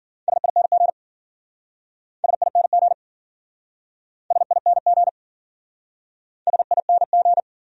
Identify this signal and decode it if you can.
Morse code, heard as a keyed tone, about 52 wpm, 700 Hz. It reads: SING SING SING SING